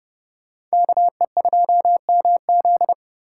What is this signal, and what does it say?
Morse code, heard as a keyed tone, about 37 wpm, 700 Hz. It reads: KE2MZ
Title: Speed Racing - USA 2x2 Call Signs 30wpm